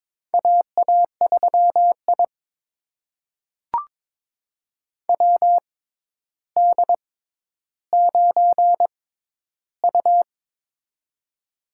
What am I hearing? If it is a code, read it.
Morse code, heard as a keyed tone, about 22 wpm, spaced as for 11 wpm, 700 Hz. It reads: AA3I WD9U
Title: USA Call Signs - 4 Characters - 6x Character Spacing - 22wpm